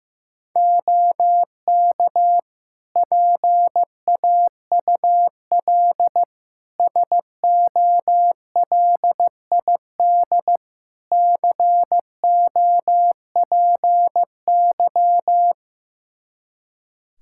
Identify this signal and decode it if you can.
Morse code, heard as a keyed tone, about 15 wpm, 700 Hz. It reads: OK PAUL SOLID COPY
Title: K1USN SST QSO Practice 15wpm